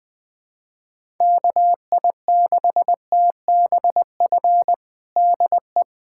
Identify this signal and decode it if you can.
Morse code, heard as a keyed tone, about 20 wpm, 700 Hz. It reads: KI6TBF DE
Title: USA Callsign DE Callsign 20wpm